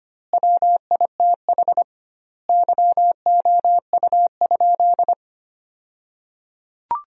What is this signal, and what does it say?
Morse code, heard as a keyed tone, about 25 wpm, 700 Hz. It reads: WITH YOU?
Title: 2x Word-Spacing - Sentences from Top 100 Words - 25wpm